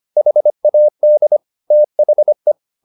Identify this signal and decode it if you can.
Morse code, heard as a keyed tone, about 25 wpm, 600 Hz. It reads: HAD THE
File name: the old man and the sea part 1 25wpm